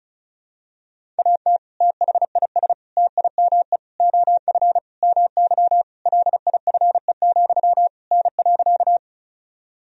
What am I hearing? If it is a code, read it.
Morse code, heard as a keyed tone, about 34 wpm, 700 Hz. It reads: AT THIS TIME OF MY LIFE, N.
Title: Sentences from Top 300 Words and Abbreviations 35wpm